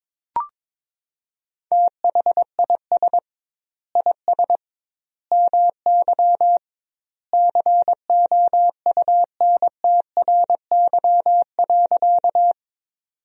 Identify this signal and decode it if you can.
Morse code, heard as a keyed tone, about 22 wpm, 700 Hz. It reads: THIS IS MY COUNTRY.
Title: 2x Word-Spacing - Sentences from Top 300 Words - 22wpm